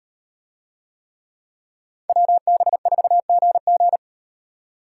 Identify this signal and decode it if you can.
Morse code, heard as a keyed tone, about 36 wpm, 700 Hz. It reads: WB4GG